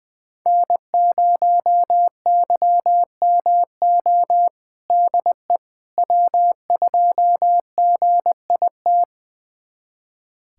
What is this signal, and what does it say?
Morse code, heard as a keyed tone, about 20 wpm, 700 Hz. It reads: N0YMO DE W2GIT
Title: USA Callsign DE Callsign - 5 Characters - 20wpm